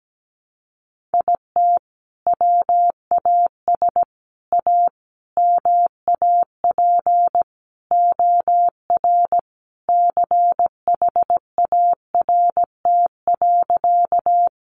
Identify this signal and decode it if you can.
Morse code, heard as a keyed tone, about 17 wpm, 700 Hz. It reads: IT WAS A MAP OR CHART.
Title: Sentences from Top 1300 Words 17wpm